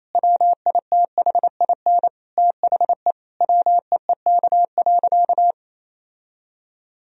Morse code, heard as a keyed tone, about 28 wpm, 700 Hz. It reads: WITHIN THE WEEK.
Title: Sentences from Top 1000 Words - Review Part 4 - 28wpm